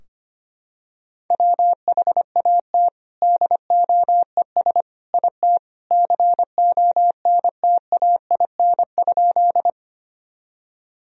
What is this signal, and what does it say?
Morse code, heard as a keyed tone, about 25 wpm, 700 Hz. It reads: WHAT DOES IT CONTAIN?
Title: Sentences from Top 1000 Words - Review Part 2 - 25wpm